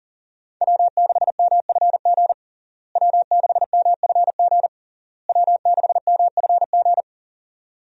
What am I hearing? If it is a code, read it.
Morse code, heard as a keyed tone, about 40 wpm, 700 Hz. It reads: W6MFG W6MFG W6MFG
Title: USA Call Signs - 5 Characters - Repeated 3x - 40wpm